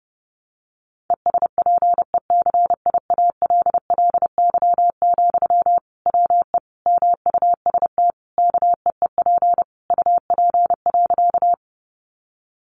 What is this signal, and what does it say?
Morse code, heard as a keyed tone, about 30 wpm, 700 Hz. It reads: ESPECIALLY, WE MUST KEEP UP.
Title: Sentences from Top 700 Words 30wpm